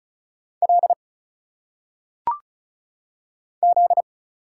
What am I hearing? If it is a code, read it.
Morse code, heard as a keyed tone, about 36 wpm, 700 Hz. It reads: L Z
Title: Single Letters - Phonetic Pronunciation 35wpm